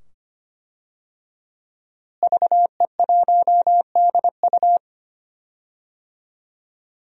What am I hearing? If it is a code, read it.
Morse code, heard as a keyed tone, about 25 wpm, 700 Hz. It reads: VE1DU
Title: DXCC-100 Call Signs - 5 Characters - Repeated 2x - 25wpm